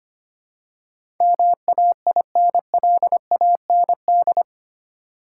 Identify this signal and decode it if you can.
Morse code, heard as a keyed tone, about 25 wpm, 700 Hz. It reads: MAINLAND
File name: Eight Letter Words - 25wpm